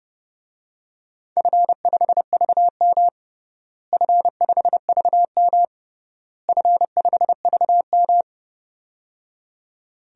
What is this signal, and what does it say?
Morse code, heard as a keyed tone, about 30 wpm, 700 Hz. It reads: F5VM F5VM F5VM
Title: DXCC-100 Call Signs - 4 Characters - Repeated 3x - 30wpm